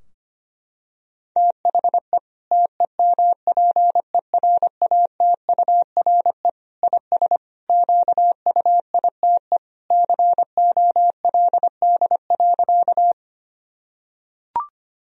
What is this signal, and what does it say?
Morse code, heard as a keyed tone, about 25 wpm, 700 Hz. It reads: THE TEMPERATURE IS QUITE COLD.
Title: Sentences from Top 1000 Words - Review Part 4 - 25wpm